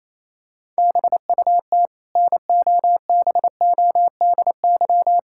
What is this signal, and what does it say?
Morse code, heard as a keyed tone, about 28 wpm, 700 Hz. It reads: BUT NOBODY
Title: Sentences from Top 1600 Words 28wpm